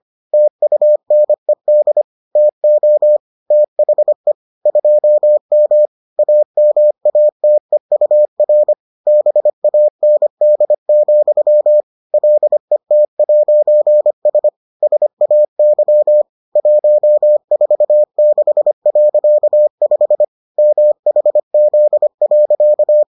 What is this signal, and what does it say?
Morse code, heard as a keyed tone, about 25 wpm, 600 Hz. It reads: TUNED TO THE 2M AMATEUR BAND, LET'S SAY 146.5 MHZ.